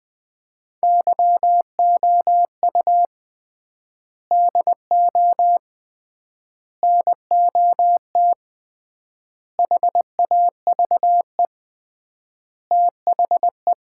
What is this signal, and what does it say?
Morse code, heard as a keyed tone, about 20 wpm, 700 Hz. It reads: YOU DO NOT HAVE THE
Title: 3x Word-Spacing - Sets of 5 Words - 20wpm